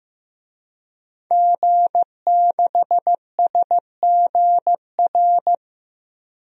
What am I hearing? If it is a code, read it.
Morse code, heard as a keyed tone, about 15 wpm, 700 Hz. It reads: G6SGR
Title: DXCC-100 Call Signs - 5 Characters - Repeated 3x - 15wpm